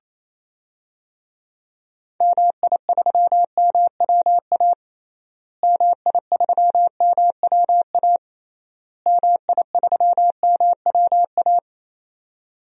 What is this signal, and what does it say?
Morse code, heard as a keyed tone, about 28 wpm, 700 Hz. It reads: MI3MWA MI3MWA MI3MWA
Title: DXCC-100 Call Signs - 6 Characters - Repeated 3x - 28wpm